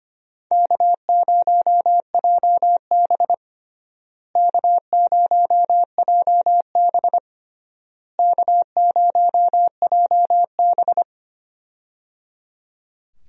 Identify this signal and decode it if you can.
Morse code, heard as a keyed tone, about 25 wpm, 700 Hz. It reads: K0JB K0JB K0JB
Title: USA Call Signs - 4 Characters - Repeated 3x - 25wpm